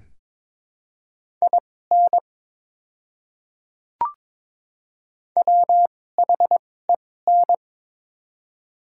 Morse code, heard as a keyed tone, about 21 wpm, spaced as for 16 wpm, 700 Hz. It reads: IN WHEN
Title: 2x Character Spacing - Top 100 Words - 22wpm